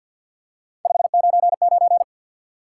Morse code, heard as a keyed tone, about 52 wpm, 700 Hz. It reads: 599
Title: Top 90 QSO Elements 50wpm